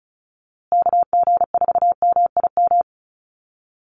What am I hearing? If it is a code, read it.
Morse code, heard as a keyed tone, about 34 wpm, 700 Hz. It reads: KG4MIM
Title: USA Call Signs Repeated 3x 35wpm